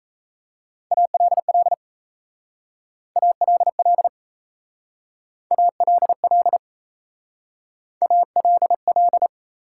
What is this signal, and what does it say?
Morse code, heard as a keyed tone, about 34 wpm, 700 Hz. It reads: ALL ALL ALL ALL
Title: Speed-Racing Top 180 QSO Elements 28wpm